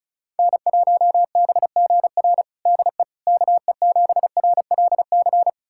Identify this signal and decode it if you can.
Morse code, heard as a keyed tone, about 34 wpm, 700 Hz. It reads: N1BGR DE KE7RLC
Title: USA Callsign DE Callsign 35wpm